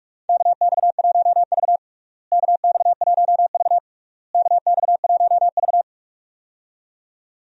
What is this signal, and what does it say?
Morse code, heard as a keyed tone, about 46 wpm, 700 Hz. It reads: KX1V KX1V KX1V
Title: USA Call Signs - 4 Characters - Repeated 3x - 45wpm